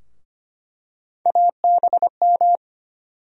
Morse code, heard as a keyed tone, about 25 wpm, 700 Hz. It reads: ABM
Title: USA Call Sign Postfixes - 3 Letters 25wpm